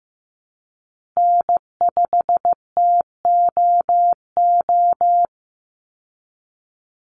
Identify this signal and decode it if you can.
Morse code, heard as a keyed tone, about 15 wpm, 700 Hz. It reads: N5TOO
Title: Call Signs Repeated 2x - 15wpm